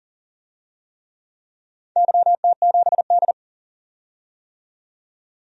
Morse code, heard as a keyed tone, about 40 wpm, 700 Hz. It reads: YT7D
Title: DXCC-100 Call Signs - 4 Characters - 40wpm